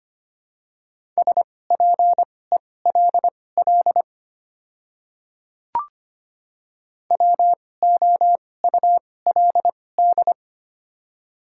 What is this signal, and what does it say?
Morse code, heard as a keyed tone, about 25 wpm, spaced as for 18 wpm, 700 Hz. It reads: SPELL WOULD
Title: 2x Character Spacing - Top 200 Words - 25wpm